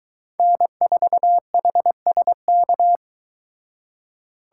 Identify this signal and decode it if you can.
Morse code, heard as a keyed tone, about 23 wpm, 700 Hz. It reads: N4HSK